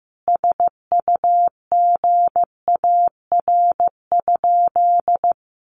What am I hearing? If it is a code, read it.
Morse code, heard as a keyed tone, about 15 wpm, 700 Hz. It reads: SUGAR?
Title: Sentences from Top 1200 Words 15wpm